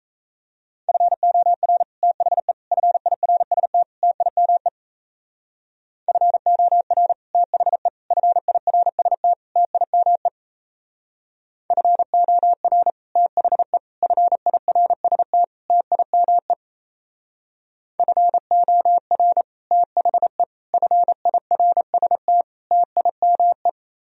Morse code, heard as a keyed tone, about 34 wpm, 700 Hz. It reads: FOR THE FIRST TIME FOR THE FIRST TIME FOR THE FIRST TIME FOR THE FIRST TIME
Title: Speed-Racing - Sets of 4 Words 28wpm